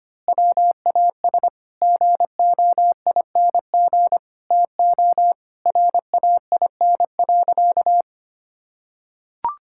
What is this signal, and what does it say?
Morse code, heard as a keyed tone, about 25 wpm, 700 Hz. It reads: WAS GOING TO RAIN.
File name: practice / 25wpm / Sentences with 4-Word Phrases - 25wpm